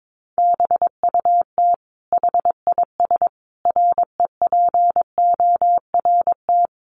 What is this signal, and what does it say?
Morse code, heard as a keyed tone, about 22 wpm, 700 Hz. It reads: BUT HIS REPORT
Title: Sentences from Top 1200 Words 22wpm